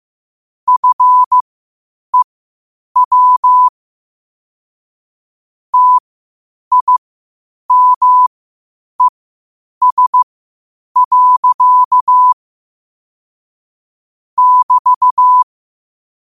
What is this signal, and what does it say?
Morse code, heard as a keyed tone, about 15 wpm, spaced as for 8 wpm, 1000 Hz. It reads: FEW TIMES. <BT>
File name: qotdfiles / 8wpm